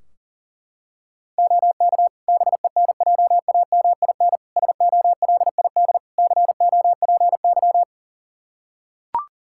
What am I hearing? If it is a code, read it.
Morse code, heard as a keyed tone, about 40 wpm, 700 Hz. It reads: OK BENJAMIN SOLID COPY
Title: K1USN SST QSO Practice 40wpm